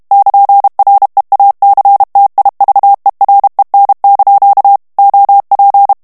Morse code, heard as a keyed tone, about 32 wpm, 800 Hz. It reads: <KN>REACTIVEREN) OP
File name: morse mp3 / HOLLANDAIS texte clair 2 vitesse 30 mots minute